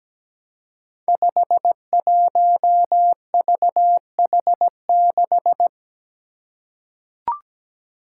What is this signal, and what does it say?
Morse code, heard as a keyed tone, about 17 wpm, 700 Hz. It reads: 51VH6
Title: Five Challenging Letter-Number Combinations 17wpm